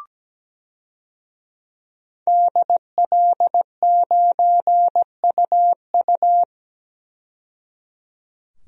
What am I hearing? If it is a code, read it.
Morse code, heard as a keyed tone, about 17 wpm, 700 Hz. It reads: DL9UU
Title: DXCC-100 Call Signs - 5 Characters - 17wpm